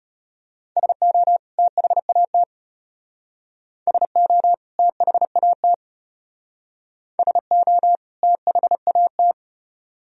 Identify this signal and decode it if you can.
Morse code, heard as a keyed tone, about 34 wpm, 700 Hz. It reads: SO THAT SO THAT SO THAT